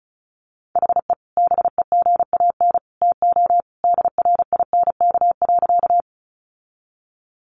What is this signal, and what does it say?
Morse code, heard as a keyed tone, about 34 wpm, 700 Hz. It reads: HE BEGAN TO DRINK.
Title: Sentences from Top 1500 Words 35wpm